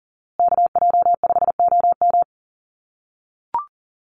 Morse code, heard as a keyed tone, about 40 wpm, 700 Hz. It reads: KJ5OM
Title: USA Call Signs Repeated 2x 40wpm